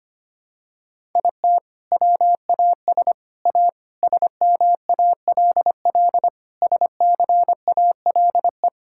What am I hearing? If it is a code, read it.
Morse code, heard as a keyed tone, about 25 wpm, 700 Hz. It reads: IT WAS A SMALL SCALE
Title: Sentences from Top 1000 Words - Review Part 5 - 25wpm